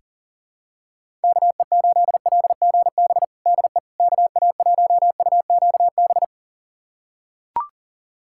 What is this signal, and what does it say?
Morse code, heard as a keyed tone, about 40 wpm, 700 Hz. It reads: KE8LGB DE KA1UQB
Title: USA Callsign DE Callsign - 6 Characters - 40wpm